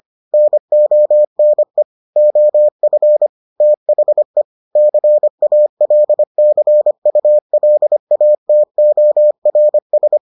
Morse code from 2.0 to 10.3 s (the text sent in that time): OF THE CALCULATORS